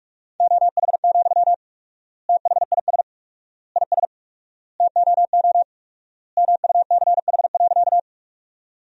Morse code, heard as a keyed tone, about 46 wpm, 700 Hz. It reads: OH, THIS IS TOO MUCH.
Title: 4x Word-Spacing - Sentences from Top 500 Words - 45wpm